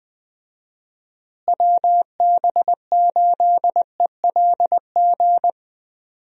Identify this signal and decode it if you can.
Morse code, heard as a keyed tone, about 20 wpm, 700 Hz. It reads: WB8ELG
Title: USA Call Signs - 6 Characters - Repeated 2x - 20wpm